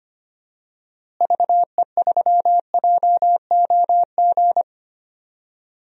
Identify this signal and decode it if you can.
Morse code, heard as a keyed tone, about 25 wpm, 700 Hz. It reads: VE3JOG
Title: DXCC-100 Call Signs - 6 Characters - Repeated 3x - 25wpm